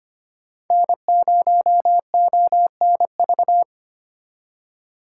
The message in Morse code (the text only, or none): N0ONV